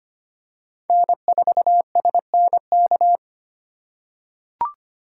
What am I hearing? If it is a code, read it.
Morse code, heard as a keyed tone, about 25 wpm, 700 Hz. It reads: N4SNK
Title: USA Call Signs - 5 Characters - 2x Character Spacing - 25wpm